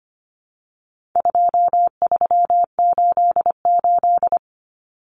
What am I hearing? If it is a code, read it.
Morse code, heard as a keyed tone, about 25 wpm, 700 Hz. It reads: 2388
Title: Numbers - 4-Digit 25wpm